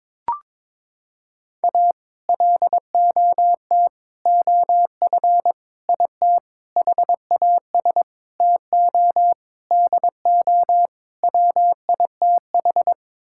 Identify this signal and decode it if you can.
Morse code, heard as a keyed tone, about 22 wpm, 700 Hz. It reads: A LOT OF IT HAS TO DO WITH
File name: practice / 22wpm / Sets of 8 Words - 22wpm